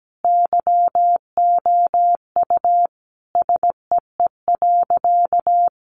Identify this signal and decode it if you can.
Morse code, heard as a keyed tone, about 17 wpm, 700 Hz. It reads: YOU SEE.
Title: Sentences from Top 100 Words 17wpm